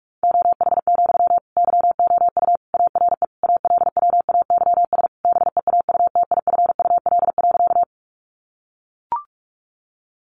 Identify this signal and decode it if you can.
Morse code, heard as a keyed tone, about 44 wpm, 700 Hz. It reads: OH, YOU ARE ALWAYS BEAUTIFUL.
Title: Sentences from Top 900 Words 45wpm